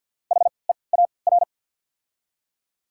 Morse code, heard as a keyed tone, about 52 wpm, spaced as for 29 wpm, 690 Hz. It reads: HEAR
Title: 3x Character Spacing - Top 300 Words - 50wpm